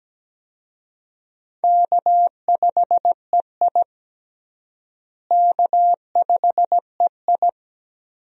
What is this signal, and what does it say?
Morse code, heard as a keyed tone, about 17 wpm, 700 Hz. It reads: K5EI K5EI
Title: USA Call Signs - 4 Characters - Repeated 3x - 17wpm